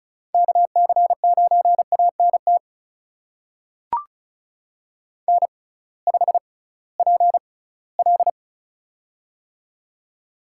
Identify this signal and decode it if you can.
Morse code, heard as a keyed tone, about 36 wpm, spaced as for 20 wpm, 700 Hz. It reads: KC9ANT N5PL
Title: USA Call Signs - 6x Character Spacing - 35wpm